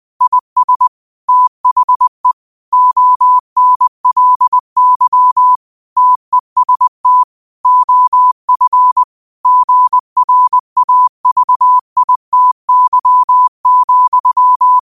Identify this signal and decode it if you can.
Morse code, heard as a keyed tone, about 20 wpm, 1000 Hz. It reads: IS THE ONLY TEST OF GRAVITY,
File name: qotdfiles / 20wpm